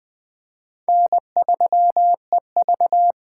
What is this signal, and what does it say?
Morse code, heard as a keyed tone, about 20 wpm, 700 Hz. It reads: N3EV
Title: USA Call Signs - 8x Character Spacing - 20wpm